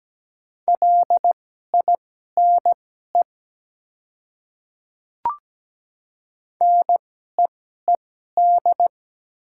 Morse code, heard as a keyed tone, about 17 wpm, spaced as for 12 wpm, 700 Hz. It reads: LINE NEED
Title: 2x Character Spacing - Top 200 Words - 17wpm